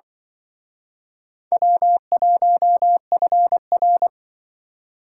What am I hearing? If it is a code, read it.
Morse code, heard as a keyed tone, about 25 wpm, 690 Hz. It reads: W1FR